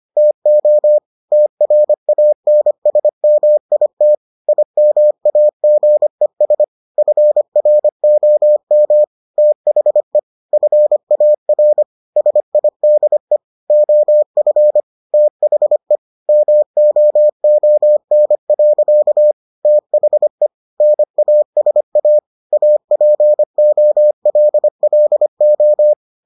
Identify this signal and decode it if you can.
Morse code, heard as a keyed tone, about 25 wpm, 600 Hz. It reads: TO TRANSMIT IMAGES FROM THE FAR SIDE OF THE MOON. THE NASA APOLLO